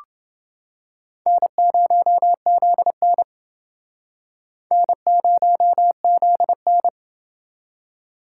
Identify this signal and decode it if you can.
Morse code, heard as a keyed tone, about 28 wpm, 700 Hz. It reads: N0ZN N0ZN